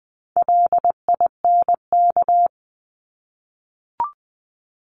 Morse code, heard as a keyed tone, about 20 wpm, 700 Hz. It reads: LINK
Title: Top 751-1000 Words 20wpm